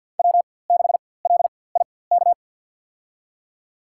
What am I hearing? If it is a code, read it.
Morse code, heard as a keyed tone, about 50 wpm, spaced as for 23 wpm, 700 Hz. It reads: W6LIK
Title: USA Call Signs - 4x Character Spacing - 50wpm